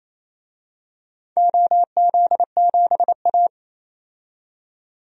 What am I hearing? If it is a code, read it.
Morse code, heard as a keyed tone, about 28 wpm, 700 Hz. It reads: OZ7A
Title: DXCC-100 Call Signs - 4 Characters - 28wpm